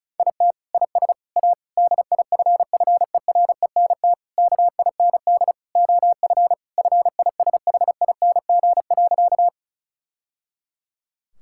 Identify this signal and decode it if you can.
Morse code, heard as a keyed tone, about 36 wpm, 700 Hz. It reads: IT IS A DIFFERENT KIND OF FISHING.
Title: Sentences from Top 1000 Words - Review Part 5 - 35wpm